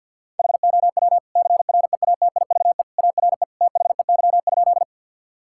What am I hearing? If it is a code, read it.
Morse code, heard as a keyed tone, about 50 wpm, 700 Hz. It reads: HOW CREATIVE ARE THEY?
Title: Sentences from Top 1000 Words - Review Part 5 - 50wpm